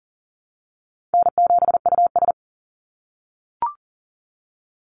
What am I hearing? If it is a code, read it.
Morse code, heard as a keyed tone, about 40 wpm, 700 Hz. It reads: N7US
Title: Easy Call Signs - 40wpm